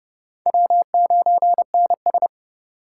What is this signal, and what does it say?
Morse code, heard as a keyed tone, about 29 wpm, 700 Hz. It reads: W9NS